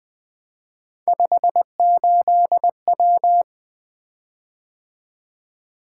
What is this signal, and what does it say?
Morse code, heard as a keyed tone, about 20 wpm, 700 Hz. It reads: 58W
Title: Three Letter-Number Combinations - 20wpm